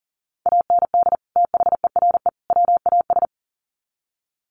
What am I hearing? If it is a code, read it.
Morse code, heard as a keyed tone, about 40 wpm, 700 Hz. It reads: AND THERE WAS
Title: Sets of 3 Words 40wpm